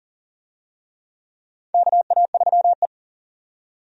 Morse code, heard as a keyed tone, about 39 wpm, 700 Hz. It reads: KA3E
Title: USA Call Signs - 4 Characters - 40wpm